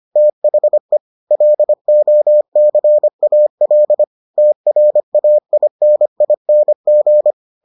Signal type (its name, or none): Morse code